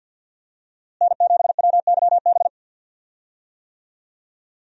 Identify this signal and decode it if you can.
Morse code, heard as a keyed tone, about 48 wpm, 700 Hz. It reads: N7WYB
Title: USA Call Signs - 5 Characters - Repeated 3x - 50wpm